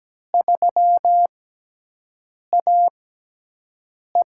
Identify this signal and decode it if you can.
Morse code, heard as a keyed tone, about 17 wpm, spaced as for 6 wpm, 700 Hz. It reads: 3AE
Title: USA Call Signs - 4 Characters - 6x Character Spacing - 17wpm